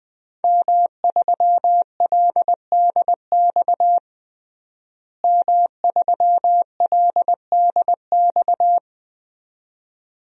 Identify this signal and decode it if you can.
Morse code, heard as a keyed tone, about 20 wpm, 700 Hz. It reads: M3LDX M3LDX
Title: DXCC-100 Call Signs - 5 Characters - Repeated 2x - 20wpm